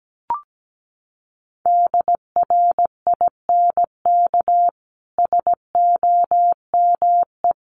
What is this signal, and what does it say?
Morse code, heard as a keyed tone, about 17 wpm, 700 Hz. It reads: DRINK SOME
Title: Sentences from Top 1500 Words 17wpm